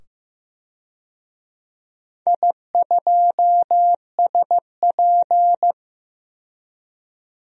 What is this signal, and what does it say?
Morse code, heard as a keyed tone, about 15 wpm, 700 Hz. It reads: I2SP
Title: DXCC-100 Call Signs - 4 Characters - 15wpm